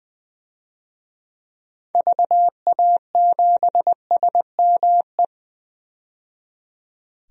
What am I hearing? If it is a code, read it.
Morse code, heard as a keyed tone, about 20 wpm, 700 Hz. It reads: VA7SME
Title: DXCC-100 Call Signs - 6 Characters - 20wpm